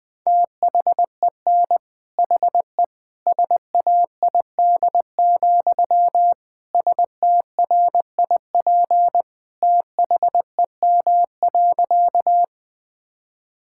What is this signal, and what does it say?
Morse code, heard as a keyed tone, about 20 wpm, 700 Hz. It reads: THEN HE SAID, STRIP THEM.
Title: Sentences from Top 1600 Words 20wpm